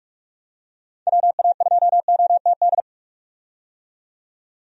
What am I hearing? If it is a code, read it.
Morse code, heard as a keyed tone, about 45 wpm, 700 Hz. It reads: WA2OTD
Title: USA Call Signs - 6 Characters - 45wpm